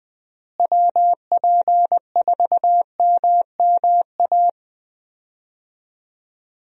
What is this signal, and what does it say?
Morse code, heard as a keyed tone, about 21 wpm, 700 Hz. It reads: WP4MMA
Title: USA Call Signs - 6 Characters - Repeated 2x - 20wpm